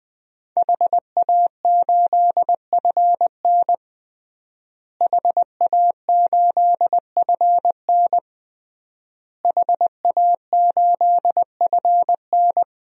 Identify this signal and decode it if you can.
Morse code, heard as a keyed tone, about 20 wpm, 700 Hz. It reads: HA8FN HA8FN HA8FN
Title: DXCC-100 Call Signs - 5 Characters - Repeated 3x - 20wpm